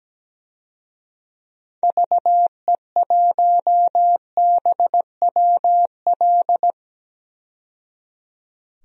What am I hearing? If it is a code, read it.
Morse code, heard as a keyed tone, about 17 wpm, 700 Hz. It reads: VE1BWL
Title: DXCC-100 Call Signs - 6 Characters - 17wpm